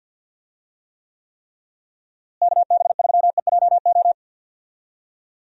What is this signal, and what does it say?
Morse code, heard as a keyed tone, about 51 wpm, 700 Hz. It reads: KB3EJO
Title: USA Call Signs - 6 Characters - Repeated 3x - 50wpm